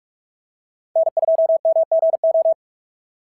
Morse code, heard as a keyed tone, about 43 wpm, 650 Hz. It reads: N2MGO
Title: SOTA QSO Practice 45wpm